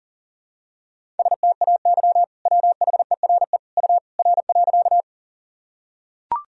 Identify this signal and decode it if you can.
Morse code, heard as a keyed tone, about 40 wpm, 700 Hz. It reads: STAY WHERE U R.
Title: Sentences from Top 500 Words and Abbreviations 40wpm